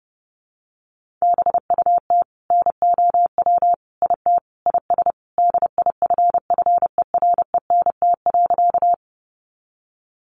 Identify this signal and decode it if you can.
Morse code, heard as a keyed tone, about 30 wpm, 700 Hz. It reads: BUT NOW IT IS DIFFERENT.
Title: Sentences from Top 600 Words 30wpm